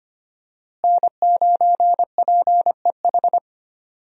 Morse code, heard as a keyed tone, about 25 wpm, 700 Hz. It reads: N9PEH
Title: USA Call Signs - 5 Characters - Repeated 2x - 25wpm